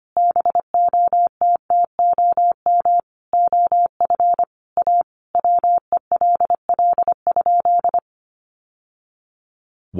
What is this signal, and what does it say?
Morse code, heard as a keyed tone, about 25 wpm, 700 Hz. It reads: BOTTOM OF A WELL?
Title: Sentences from Top 900 Words 25wpm